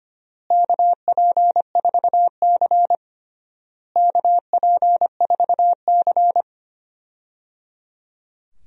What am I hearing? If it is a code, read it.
Morse code, heard as a keyed tone, about 25 wpm, 700 Hz. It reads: KP4C KP4C
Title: DXCC-100 Call Signs - 4 Characters - Repeated 3x - 25wpm